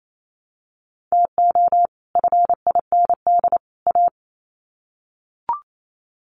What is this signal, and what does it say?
Morse code, heard as a keyed tone, about 28 wpm, 700 Hz. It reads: TO FIND A
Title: Sets of 3 Words 28wpm